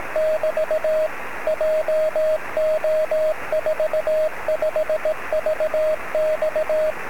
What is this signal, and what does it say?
Morse code, heard as a keyed tone, about 18 wpm, 630 Hz, with radio noise. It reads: <BT> JO45VX
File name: sm7lcb / 25-oz9ky-23cm